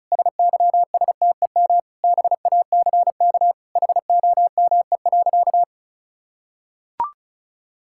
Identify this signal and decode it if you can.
Morse code, heard as a keyed tone, about 36 wpm, 700 Hz. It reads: SYSTEM BACK HOME.
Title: Sentences from Top 1000 Words - Review Part 5 - 35wpm